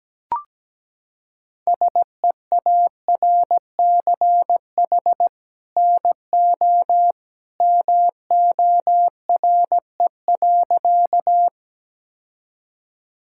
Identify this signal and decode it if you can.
Morse code, heard as a keyed tone, about 17 wpm, 700 Hz. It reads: SEARCH NO MORE.
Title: Sentences from Top 1000 Words - Review Part 5 - 17wpm